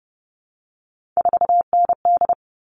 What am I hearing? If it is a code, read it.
Morse code, heard as a keyed tone, about 30 wpm, 700 Hz. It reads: 4ND
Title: All Cut Numbers - 3-Digit 30wpm